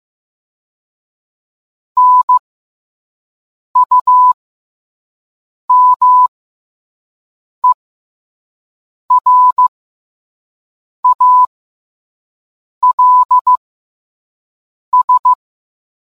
Morse code, heard as a keyed tone, about 15 wpm, spaced as for 5 wpm, 1000 Hz. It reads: NUMERALS